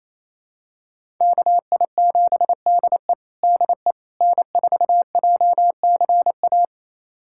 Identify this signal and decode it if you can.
Morse code, heard as a keyed tone, about 28 wpm, 690 Hz. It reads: KI7DE DE N4JCA
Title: USA Callsign DE Callsign - 5 Characters - 28wpm